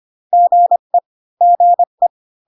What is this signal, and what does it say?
Morse code, heard as a keyed tone, about 25 wpm, 700 Hz. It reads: GE GE